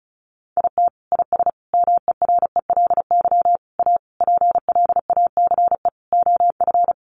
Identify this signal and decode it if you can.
Morse code, heard as a keyed tone, about 34 wpm, 700 Hz. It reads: IT IS MERELY A PLACE OF
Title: Sentences from Top 1400 Words 35wpm